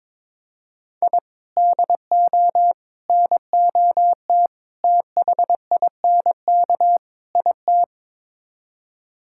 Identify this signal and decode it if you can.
Morse code, heard as a keyed tone, about 21 wpm, 700 Hz. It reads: I DO NOT THINK IT
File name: practice / 22wpm / Build A Phrase - 5-Word Phrases - 22wpm